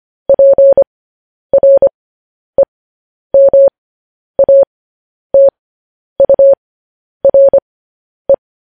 Morse code, heard as a keyed tone, about 25 wpm, spaced as for 10 wpm, 550 Hz. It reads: PREMATURE
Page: news-headlines-in-morse-code-at-10-wpm